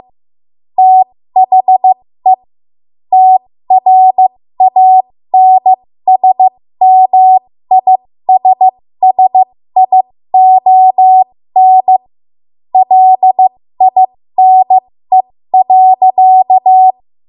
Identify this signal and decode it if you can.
Morse code, heard as a keyed tone, about 15 wpm, spaced as for 13 wpm, 750 Hz. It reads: THE TRANSMISSION LINE.
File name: Morse / 13WPM